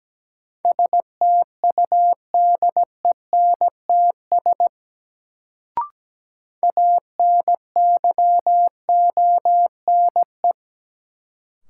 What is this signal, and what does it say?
Morse code, heard as a keyed tone, about 17 wpm, 700 Hz. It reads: STUDENTS ANYONE
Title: Sets of 5 Words Encore - New Word Drill - 17wpm